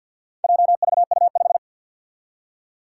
Morse code, heard as a keyed tone, about 51 wpm, 700 Hz. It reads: JVU5
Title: Four Challenging Letter-Number Combinations 50wpm